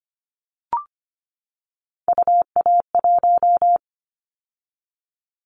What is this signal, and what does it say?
Morse code, heard as a keyed tone, about 25 wpm, 700 Hz. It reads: UA1
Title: Callsign Prefixes for Top 25 DXCC - 25wpm